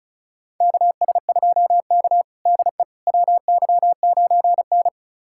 Morse code, heard as a keyed tone, about 34 wpm, 700 Hz. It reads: KS2K DE WY9N
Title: USA Callsign DE Callsign - 4 Characters - 35wpm